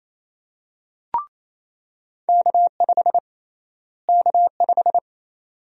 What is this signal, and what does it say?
Morse code, heard as a keyed tone, about 28 wpm, 700 Hz. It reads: K5 K5
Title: USA Call Sign Prefixes - Repeated 2x - 28wpm